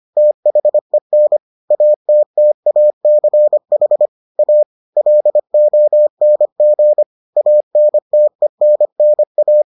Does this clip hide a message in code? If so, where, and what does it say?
Yes: Morse code from 0.2 to 9.6 s, heard as a keyed tone, about 25 wpm, 600 Hz. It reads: THEN ATTACH A LONG ANTENNA